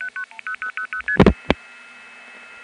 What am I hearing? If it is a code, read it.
Multi-frequency trunk signalling, heard as fast pairs of tones, tones of about 90 ms. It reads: 10000ST